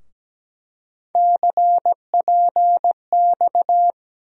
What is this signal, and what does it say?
Morse code, heard as a keyed tone, about 17 wpm, 700 Hz. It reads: CPX
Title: USA Call Sign Postfixes - 3 Letters 17wpm